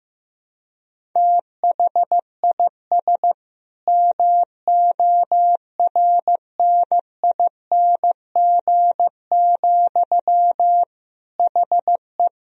Morse code, heard as a keyed tone, about 15 wpm, 700 Hz. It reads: THIS MORNING, HE
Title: Sentences from Top 1000 Words - Review Part 1 - 15wpm